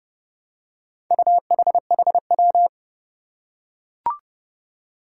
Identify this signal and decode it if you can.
Morse code, heard as a keyed tone, about 30 wpm, 700 Hz. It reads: UHHW
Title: Four Challenging Letter-Number Combinations 30wpm